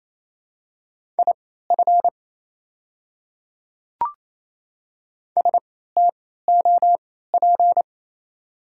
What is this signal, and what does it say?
Morse code, heard as a keyed tone, about 28 wpm, spaced as for 16 wpm, 690 Hz. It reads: IF STOP
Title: 3x Character Spacing - Top 500 Words - 28wpm